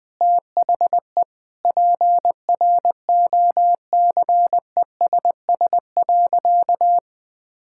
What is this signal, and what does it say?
Morse code, heard as a keyed tone, about 19 wpm, 700 Hz. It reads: THE PROCESS.
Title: Sentences with 2-Word Phrases - Spoken Beginning - 20wpm